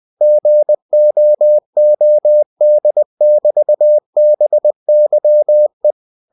Morse code, heard as a keyed tone, about 20 wpm, 600 Hz. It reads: GOOD<BT>BYE
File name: cw-books / winnie-the-pooh / winnie-20wpm-page-0109